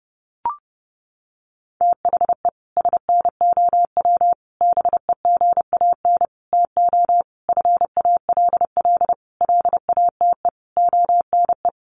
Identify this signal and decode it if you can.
Morse code, heard as a keyed tone, about 30 wpm, 700 Hz. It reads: THE SNOW BEGAN TO FALL LATE ONE
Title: Sentences from Top 1500 Words 30wpm